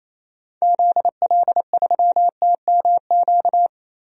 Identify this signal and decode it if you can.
Morse code, heard as a keyed tone, about 28 wpm, 700 Hz. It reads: ZL3TMQ
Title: DXCC-100 Call Signs - 6 Characters - 28wpm